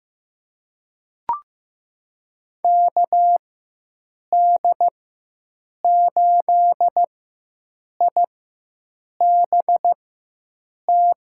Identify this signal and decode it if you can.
Morse code, heard as a keyed tone, about 15 wpm, spaced as for 7 wpm, 700 Hz. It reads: KD8IBT
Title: USA Call Signs - 6 Characters - 4x Character Spacing - 15wpm